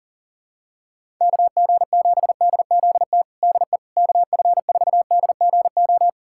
Morse code, heard as a keyed tone, about 40 wpm, 700 Hz. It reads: KG7DZT DE KF4DGO
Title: USA Callsign DE Callsign - 6 Characters - 40wpm